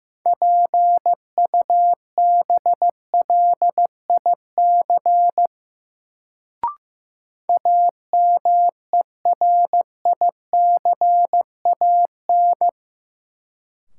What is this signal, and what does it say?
Morse code, heard as a keyed tone, about 15 wpm, 700 Hz. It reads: PUBLIC AMERICAN
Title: Sets of 2 Words - New Word Drill - 15wpm